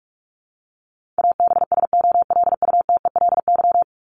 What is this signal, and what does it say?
Morse code, heard as a keyed tone, about 44 wpm, 700 Hz. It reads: ABSOLUTELY
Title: LY Suffix - Top 100 Words with Base Words 45wpm